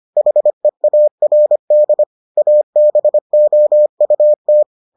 Morse code, heard as a keyed tone, about 25 wpm, 600 Hz. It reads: HEARD ABOUT